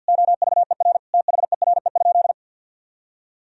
Morse code, heard as a keyed tone, about 51 wpm, 700 Hz. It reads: OVER THERE?
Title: Sentences from Top 1000 Words - Review Part 3 - 50wpm